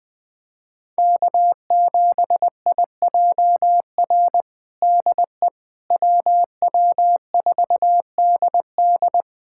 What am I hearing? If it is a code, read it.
Morse code, heard as a keyed tone, about 20 wpm, 700 Hz. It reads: K7IJR DE WW4DD
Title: USA Callsign DE Callsign - 5 Characters - 20wpm